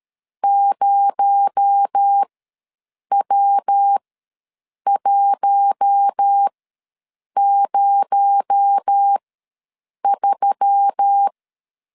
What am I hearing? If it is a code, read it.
Morse code, heard as a keyed tone, about 13 wpm, spaced as for 7 wpm, 800 Hz. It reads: 0W103
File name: LEZIONI / 6 PKX0 / 6 PKX0 ws 12wpm 3448